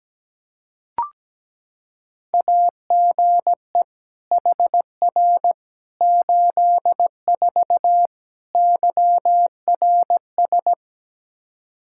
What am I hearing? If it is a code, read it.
Morse code, heard as a keyed tone, about 17 wpm, 700 Hz. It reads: AGE HR 84 YRS
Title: Practice QSO - Exchange 3 - More Info - 17wpm